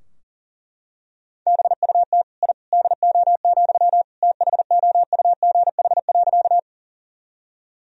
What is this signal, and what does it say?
Morse code, heard as a keyed tone, about 40 wpm, 700 Hz. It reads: BUT I DO, THOUGH.
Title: Sentences from Top 1000 Words - Review Part 2 - 40wpm